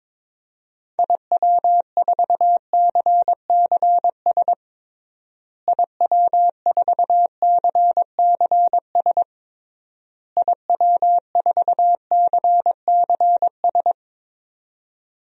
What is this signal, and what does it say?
Morse code, heard as a keyed tone, about 22 wpm, 700 Hz. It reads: IW4CCS IW4CCS IW4CCS
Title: DXCC-100 Call Signs - 6 Characters - Repeated 3x - 22wpm